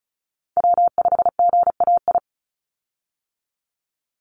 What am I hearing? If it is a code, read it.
Morse code, heard as a keyed tone, about 34 wpm, 700 Hz. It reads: W5GAI
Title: Call Signs Repeated 2x - 35wpm